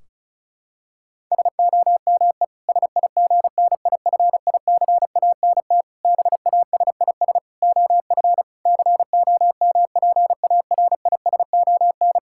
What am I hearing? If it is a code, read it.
Morse code, heard as a keyed tone, about 36 wpm, 700 Hz. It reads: SOME SIGNIFICANT BASIS OF COMPARISON
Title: Sentences from Top 1600 Words 35wpm